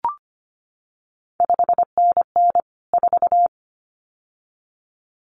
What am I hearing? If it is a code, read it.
Morse code, heard as a keyed tone, about 25 wpm, 700 Hz. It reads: 5NN 4
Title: CQ WW DX Contest Exchange - 25wpm